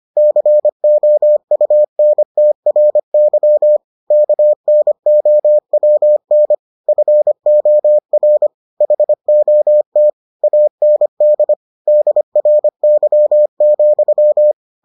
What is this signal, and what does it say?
Morse code, heard as a keyed tone, about 25 wpm, 600 Hz. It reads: COUNTRY KNOWN FOR HOT AND DRY,